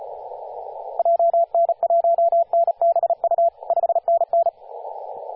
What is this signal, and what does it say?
Morse code, heard as a keyed tone, about 35 wpm, 680 Hz, with radio noise. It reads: JN1NBU 5NN
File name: posts / 2016-sep-22-0537z-kh0-kw2x-18-077mhz-cw